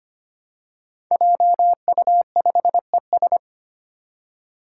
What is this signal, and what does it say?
Morse code, heard as a keyed tone, about 25 wpm, 700 Hz. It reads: JU5ES
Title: Five Challenging Letter-Number Combinations 25wpm